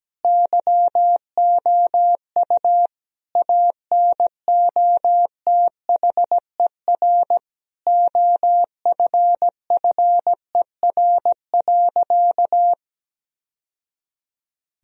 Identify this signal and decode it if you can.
Morse code, heard as a keyed tone, about 17 wpm, 700 Hz. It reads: YOU ANOTHER OFFER.
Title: Sentences from Top 1000 Words - Review Part 4 - 17wpm